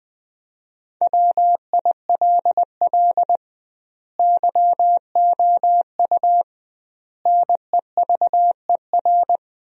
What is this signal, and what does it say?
Morse code, heard as a keyed tone, about 20 wpm, 700 Hz. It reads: WILL YOU NEVER
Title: 2x Word-Spacing - Sentences from Top 300 Words - 20wpm